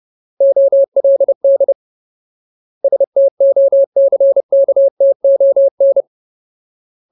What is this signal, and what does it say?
Morse code, heard as a keyed tone, about 30 wpm, 550 Hz. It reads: OLD STOCKTON
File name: Ultimate Word List / 30WPM Ultimate